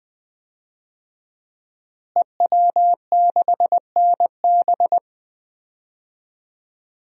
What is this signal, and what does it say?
Morse code, heard as a keyed tone, about 21 wpm, 700 Hz. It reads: EW6NB
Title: DXCC-100 Call Signs - 5 Characters - Repeated 3x - 20wpm